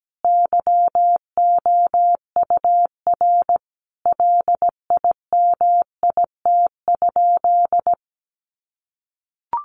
Morse code, heard as a keyed tone, about 17 wpm, 700 Hz. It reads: YOUR LIMIT?
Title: Sentences from Top 1500 Words 17wpm